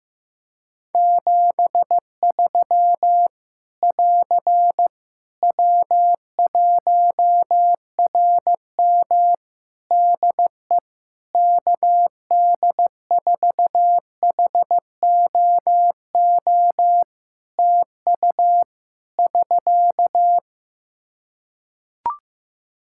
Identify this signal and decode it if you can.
Morse code, heard as a keyed tone, about 15 wpm, 700 Hz. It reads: 73 <AR> W1RM DE KD4HOO TU <SK>
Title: Practice QSO - Exchange 4 - Ending - 15wpm